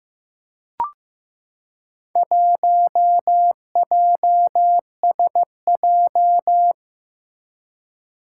Morse code, heard as a keyed tone, about 15 wpm, 700 Hz. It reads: 1JSJ
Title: Four Challenging Letter-Number Combinations 15wpm